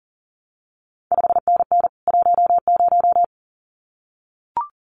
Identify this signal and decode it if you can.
Morse code, heard as a keyed tone, about 41 wpm, 700 Hz. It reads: 5NN 10
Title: CQ WW DX Contest Exchange Encore - 40wpm